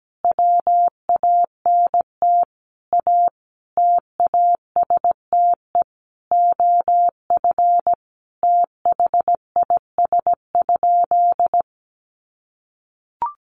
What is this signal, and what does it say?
Morse code, heard as a keyed tone, about 17 wpm, 700 Hz. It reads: WANT A TASTE OF THIS?
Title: Sentences from Top 1100 Words 17wpm